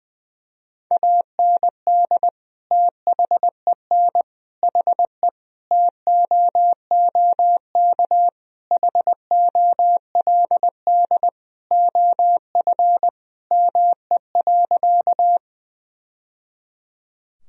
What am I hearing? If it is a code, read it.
Morse code, heard as a keyed tone, about 20 wpm, 700 Hz. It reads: AND THEN HE TOOK HOLD OF ME.
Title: Sentences from Top 1000 Words - Review Part 2 - 20wpm